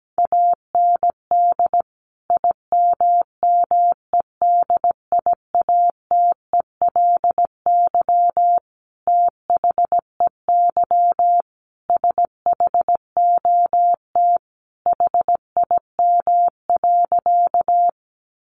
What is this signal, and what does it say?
Morse code, heard as a keyed tone, about 17 wpm, 700 Hz. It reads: AND IMMEDIATELY THEY SHOT HIM.
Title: Sentences from Top 1100 Words 17wpm